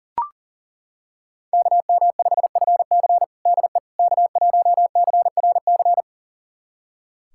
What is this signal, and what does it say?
Morse code, heard as a keyed tone, about 37 wpm, 700 Hz. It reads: KM5FC DE K1CRC
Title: USA Callsign DE Callsign - 5 Characters - 40wpm